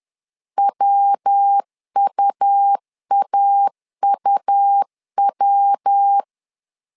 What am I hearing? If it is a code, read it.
Morse code, heard as a keyed tone, about 10 wpm, 800 Hz. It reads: WUAUW
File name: LEZIONI / 3 MWRO / 3 MW 10wpm 1959